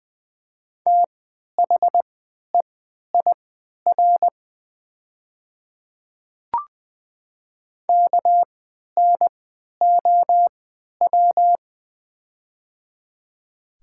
Morse code, heard as a keyed tone, about 20 wpm, spaced as for 11 wpm, 700 Hz. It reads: THEIR KNOW
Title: 3x Character Spacing - Top 100 Words - 20wpm